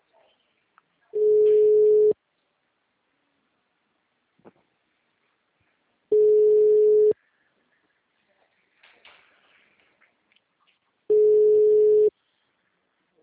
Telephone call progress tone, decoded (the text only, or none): ringback tone